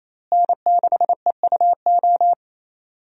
Morse code, heard as a keyed tone, about 28 wpm, 700 Hz. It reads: N6EUO
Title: USA Call Signs - 5 Characters - Repeated 3x - 28wpm